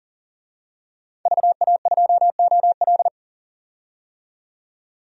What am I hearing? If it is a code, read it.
Morse code, heard as a keyed tone, about 40 wpm, 700 Hz. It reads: VA2OL
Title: DXCC-100 Call Signs - 5 Characters - 40wpm